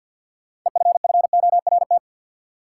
Morse code, heard as a keyed tone, about 53 wpm, 700 Hz. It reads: EFFORT
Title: Sentences from Top 1200 Words - New Word Drill Alt - 50wpm